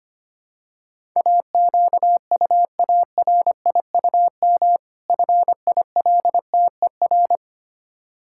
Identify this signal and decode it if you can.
Morse code, heard as a keyed tone, about 25 wpm, 700 Hz. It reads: AQUARIUM FILTER
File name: practice / 25wpm / Difficult Words with Hints - 25wpm